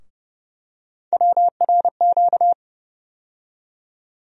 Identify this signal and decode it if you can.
Morse code, heard as a keyed tone, about 30 wpm, 700 Hz. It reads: WRQ
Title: USA Call Sign Postfixes - 3 Letters 30wpm